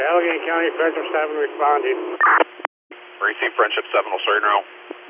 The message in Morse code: none

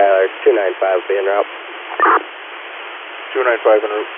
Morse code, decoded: none

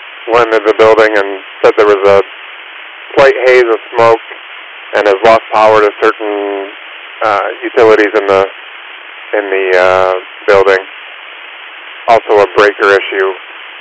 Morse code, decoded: none

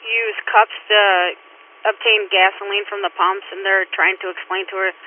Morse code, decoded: none

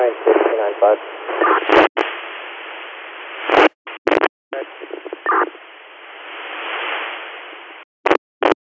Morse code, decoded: none

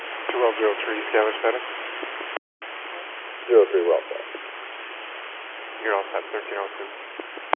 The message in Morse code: none

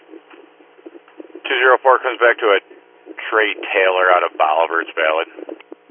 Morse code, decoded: none

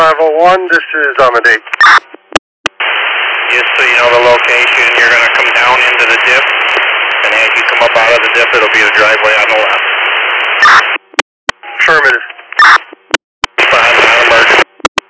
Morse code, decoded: none

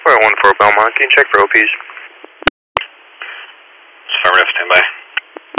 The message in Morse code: none

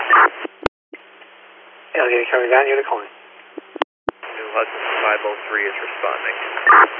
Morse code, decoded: none

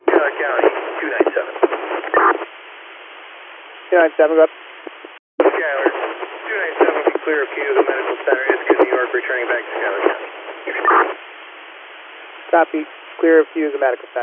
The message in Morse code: none

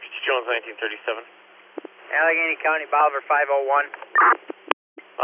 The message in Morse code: none

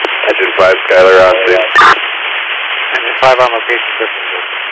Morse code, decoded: none

